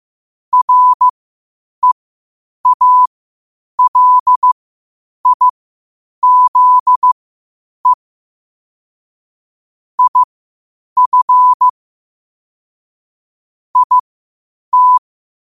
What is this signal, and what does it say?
Morse code, heard as a keyed tone, about 15 wpm, spaced as for 8 wpm, 1000 Hz. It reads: REALIZE IF IT
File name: qotdfiles / 8wpm